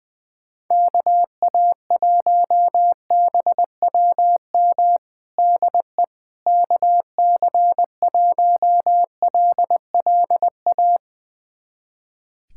Morse code, heard as a keyed tone, about 20 wpm, 700 Hz. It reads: KA1BWM DE KC1LLA
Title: USA Callsign DE Callsign 20wpm